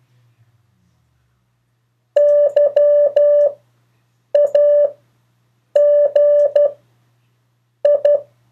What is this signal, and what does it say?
Morse code, heard as a keyed tone, about 12 wpm, spaced as for 6 wpm, 580 Hz. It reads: YAGI
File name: 3of4-words-2